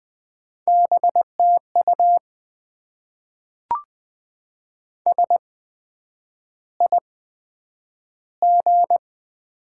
Morse code, heard as a keyed tone, about 20 wpm, spaced as for 9 wpm, 700 Hz. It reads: BTU SIG
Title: Top 90 QSO Elements - 8x Character Spacing - 20wpm